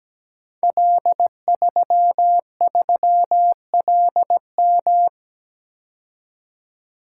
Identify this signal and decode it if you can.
Morse code, heard as a keyed tone, about 17 wpm, 700 Hz. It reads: L33LM
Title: Code Groups - 5 Letter:Numbers - 17wpm